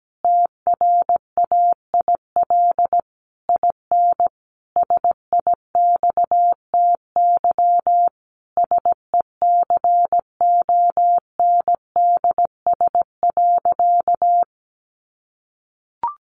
Morse code, heard as a keyed tone, about 17 wpm, 700 Hz. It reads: TRAIL IN SIXTY SECONDS.
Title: A Sound Of Thunder 17wpm